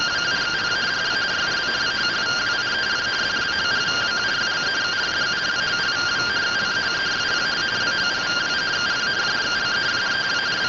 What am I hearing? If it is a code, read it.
Two-tone teleprinter signal, about 170 Hz shift.